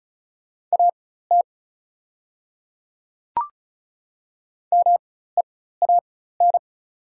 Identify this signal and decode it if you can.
Morse code, heard as a keyed tone, about 36 wpm, spaced as for 16 wpm, 700 Hz. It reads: AT MEAN
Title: 4x Character Spacing - Top 500 Words - 35wpm